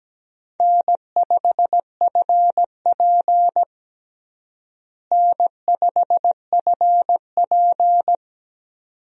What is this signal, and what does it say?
Morse code, heard as a keyed tone, about 17 wpm, 700 Hz. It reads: N5FP N5FP
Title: USA Call Signs - 4 Characters - Repeated 2x - 17wpm